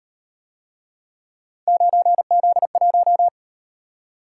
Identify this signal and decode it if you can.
Morse code, heard as a keyed tone, about 38 wpm, 700 Hz. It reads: 9Z1